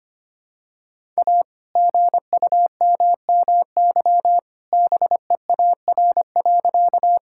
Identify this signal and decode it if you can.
Morse code, heard as a keyed tone, about 25 wpm, 700 Hz. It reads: A GUMMY BEAR.
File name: practice / 25wpm / All-Age Appropriate Jokes - 25wpm